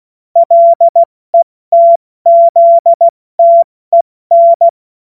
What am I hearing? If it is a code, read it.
Morse code, heard as a keyed tone, about 16 wpm, spaced as for 14 wpm, 680 Hz. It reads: LETZTEN
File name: Morsekurs / Witze / Witz 31. Mai 2022 70 F80